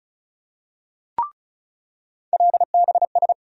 Morse code, heard as a keyed tone, about 36 wpm, 700 Hz. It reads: LBS
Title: USA Call Sign Postfixes - 3 Letters 35wpm